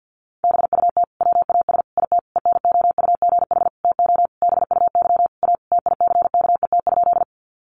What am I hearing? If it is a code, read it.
Morse code, heard as a keyed tone, about 48 wpm, 700 Hz. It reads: BUT WAS IT ENOUGH TO BUY A TICKET?
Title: Sentences from Top 1400 Words 50wpm